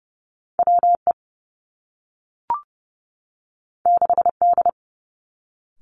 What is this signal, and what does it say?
Morse code, heard as a keyed tone, about 30 wpm, 700 Hz. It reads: W E 6 D
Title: Two Challenging Letter-Number Combinations 30wpm